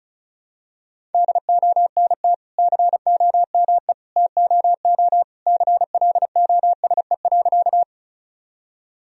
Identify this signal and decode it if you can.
Morse code, heard as a keyed tone, about 34 wpm, 700 Hz. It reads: DONT COME TOO CLOSE.
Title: Sentences from Top 1000 Words - Review Part 1 - 35wpm